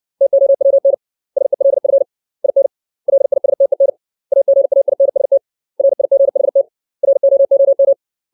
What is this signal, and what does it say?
Morse code, heard as a keyed tone, about 60 wpm, 550 Hz. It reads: TOWN SELF IN LISTEN AGAINST RIGHT ROOM
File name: 632 words / 632 Words 60WPM